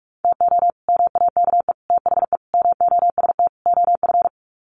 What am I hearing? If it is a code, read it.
Morse code, heard as a keyed tone, about 44 wpm, 700 Hz. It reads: TO MAKE THE MOST OF
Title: Sets of 5 Words Encore2 45wpm